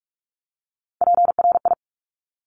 Morse code, heard as a keyed tone, about 46 wpm, 700 Hz. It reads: PRI
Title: Top Three Letter Combinations in Words 45wpm